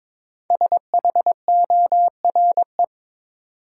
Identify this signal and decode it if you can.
Morse code, heard as a keyed tone, about 22 wpm, 700 Hz. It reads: SHORE
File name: 5 Letter Words - First 3 Letters Given - 22wpm